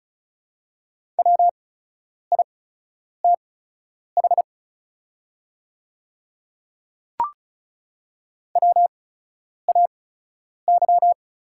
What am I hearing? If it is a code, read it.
Morse code, heard as a keyed tone, about 36 wpm, spaced as for 10 wpm, 690 Hz. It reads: WITH WAY
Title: 8x Character Spacing - Top 200 Words - 35wpm